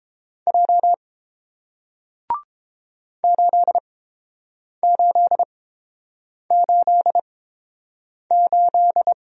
Morse code, heard as a keyed tone, about 30 wpm, spaced as for 7 wpm, 700 Hz. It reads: J 8888